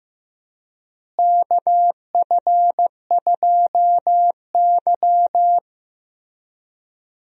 Morse code, heard as a keyed tone, about 15 wpm, 700 Hz. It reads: KF2Y
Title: USA Call Signs - 4 Characters - Repeated 2x - 15wpm